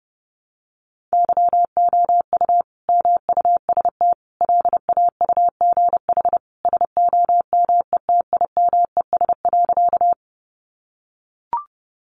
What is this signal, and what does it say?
Morse code, heard as a keyed tone, about 30 wpm, 700 Hz. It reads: YOU MUST LAUGH SOMETIMES.
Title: Sentences from Top 600 Words 30wpm